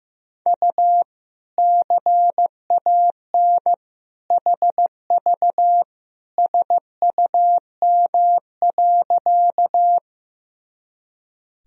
Morse code, heard as a keyed tone, about 15 wpm, 700 Hz. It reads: U CAN HV SUM.
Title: Sentences from Top 100 Words and Abbreviations 15wpm